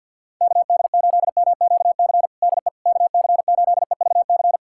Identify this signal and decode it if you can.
Morse code, heard as a keyed tone, about 46 wpm, 700 Hz. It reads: KD8MQC DE KC8EVC
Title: USA Callsign DE Callsign 50wpm